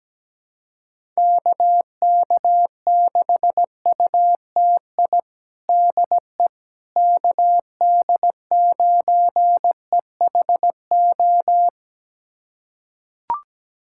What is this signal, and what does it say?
Morse code, heard as a keyed tone, about 17 wpm, 700 Hz. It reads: KK6UTI DE KD9EHO
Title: USA Callsign DE Callsign 17wpm